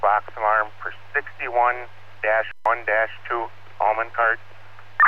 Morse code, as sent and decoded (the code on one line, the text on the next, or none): none